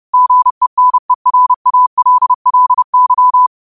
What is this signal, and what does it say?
Morse code, heard as a keyed tone, about 30 wpm, 1000 Hz. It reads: GENERALLY